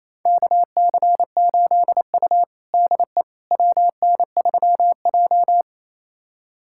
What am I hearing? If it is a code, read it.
Morse code, heard as a keyed tone, about 28 wpm, 700 Hz. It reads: KC8U DE WN3J
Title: USA Callsign DE Callsign - 4 Characters - 28wpm